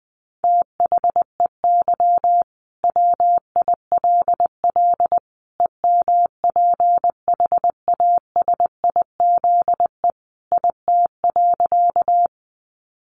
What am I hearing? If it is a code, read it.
Morse code, heard as a keyed tone, about 20 wpm, 700 Hz. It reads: THEY WILL EMPHASIZE IT.
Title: Sentences from Top 1500 Words 20wpm